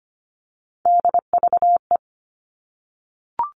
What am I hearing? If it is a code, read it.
Morse code, heard as a keyed tone, about 25 wpm, 700 Hz. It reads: DVE
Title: All Cut Numbers - 3-Digit 25wpm